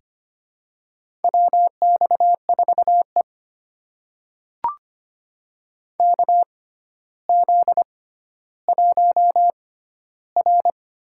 Morse code, heard as a keyed tone, about 25 wpm, spaced as for 13 wpm, 700 Hz. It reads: WX4E KZ1R
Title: USA Call Signs - 4 Characters - 6x Character Spacing - 25wpm